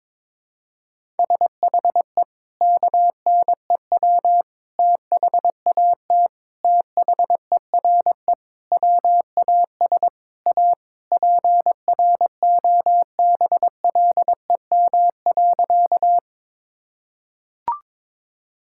Morse code, heard as a keyed tone, about 22 wpm, 700 Hz. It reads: SHE KNEW THAT THERE WAS A PROBLEM.
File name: practice / 22wpm / Sentences with 4-Word Phrases - 22wpm